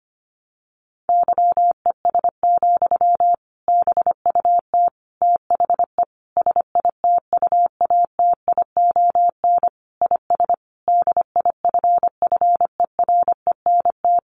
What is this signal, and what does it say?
Morse code, heard as a keyed tone, about 25 wpm, 700 Hz. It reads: YES, BUT THE SITUATION IS DIFFERENT
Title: Sentences from Top 1200 Words 25wpm